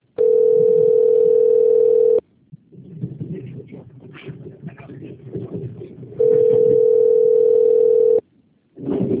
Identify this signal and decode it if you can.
Ringback tone (North America), 440 + 480 Hz.
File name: rg-1600-unknown- - - .5455